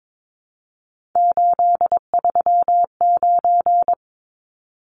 Morse code, heard as a keyed tone, about 22 wpm, 700 Hz. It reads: 839
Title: Numbers - 3-Digit 22wpm